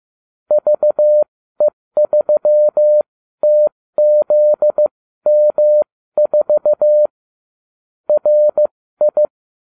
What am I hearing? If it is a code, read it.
Morse code, heard as a keyed tone, about 15 wpm, spaced as for 13 wpm, 600 Hz. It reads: VE3TZM4 RI